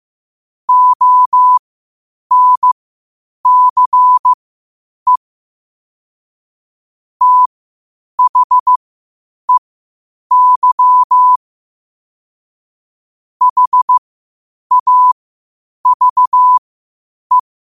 Morse code, heard as a keyed tone, about 15 wpm, spaced as for 8 wpm, 1000 Hz. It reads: ONCE THEY HAVE
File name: qotdfiles / 8wpm